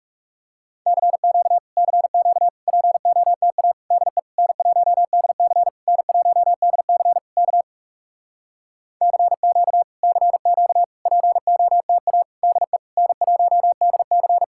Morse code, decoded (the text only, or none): CQ CQ POTA DE N1DC N1DC K CQ CQ POTA DE N1DC